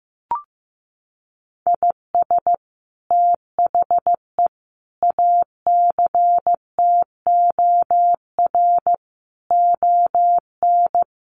Morse code, heard as a keyed tone, about 14 wpm, 700 Hz. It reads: IS THE ACTOR ON
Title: Sentences from Top 1500 Words 15wpm